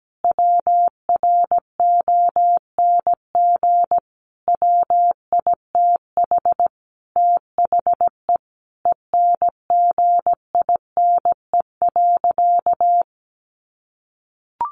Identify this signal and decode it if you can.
Morse code, heard as a keyed tone, about 17 wpm, 700 Hz. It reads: WRONG WITH THE ENGINE.
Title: Sentences from Top 1500 Words 17wpm